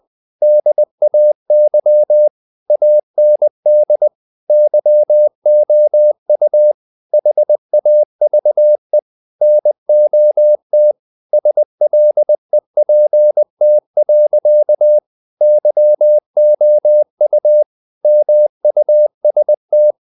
Morse code, heard as a keyed tone, about 20 wpm, 600 Hz. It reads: DAY AND YOU HAVE NOT SLEPT. YOU MUST
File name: the old man and the sea part 2 20wpm